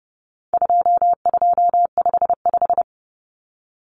Morse code, heard as a keyed tone, about 30 wpm, 700 Hz. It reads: 2255
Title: Numbers - 4-Digit 30wpm